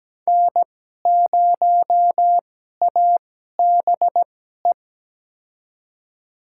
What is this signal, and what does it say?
Morse code, heard as a keyed tone, about 17 wpm, spaced as for 12 wpm, 700 Hz. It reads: N0ABE
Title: USA Call Signs - 5 Characters - 2x Character Spacing - 17wpm